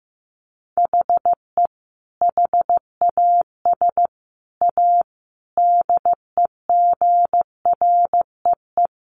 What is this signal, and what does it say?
Morse code, heard as a keyed tone, about 15 wpm, 700 Hz. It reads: HE HAS A DEGREE
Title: Sentences from Top 1500 Words 15wpm